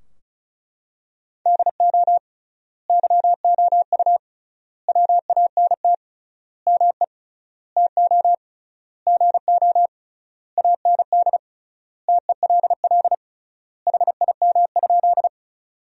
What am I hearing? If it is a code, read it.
Morse code, heard as a keyed tone, about 36 wpm, 700 Hz. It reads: DO YOU WANT ME TO GO AND TELL HIM?
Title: 3x Word-Spacing - Sentences from Top 200 Words - 35wpm